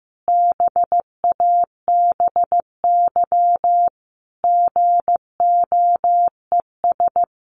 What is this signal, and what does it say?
Morse code, heard as a keyed tone, about 15 wpm, 700 Hz. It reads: BABY GOES
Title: Easy Phrases 15wpm